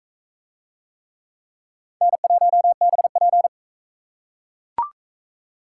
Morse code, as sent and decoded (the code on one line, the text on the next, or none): -. .---- -... .--.
N1BP